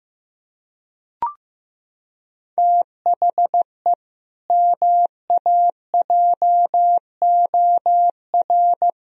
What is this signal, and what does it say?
Morse code, heard as a keyed tone, about 16 wpm, 700 Hz. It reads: THE MAJOR
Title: Sentences from Top 1000 Words - Review Part 4 - 15wpm